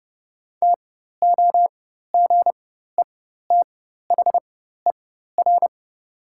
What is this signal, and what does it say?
Morse code, heard as a keyed tone, about 30 wpm, spaced as for 14 wpm, 700 Hz. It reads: TOGETHER
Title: 4x Character Spacing - Top 300 Words - 30wpm